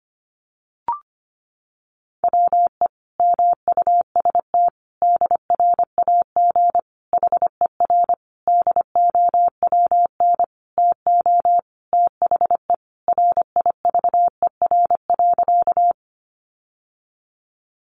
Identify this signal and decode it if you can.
Morse code, heard as a keyed tone, about 25 wpm, 700 Hz. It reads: WE MUST DRAG HER DOWN TO THE RIVER.
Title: Sentences from Top 1500 Words 25wpm